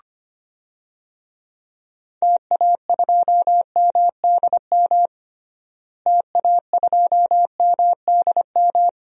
Morse code, heard as a keyed tone, about 25 wpm, 700 Hz. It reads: TA2MDM TA2MDM
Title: DXCC-100 Call Signs - 6 Characters - Repeated 2x - 25wpm